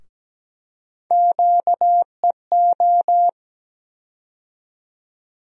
Morse code, heard as a keyed tone, about 17 wpm, 700 Hz. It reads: QEO
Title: USA Call Sign Postfixes - 3 Letters 17wpm